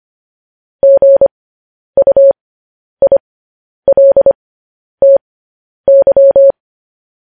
Morse code, heard as a keyed tone, about 25 wpm, spaced as for 10 wpm, 550 Hz. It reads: GUILTY
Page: news headlines in morse code at 10 wpm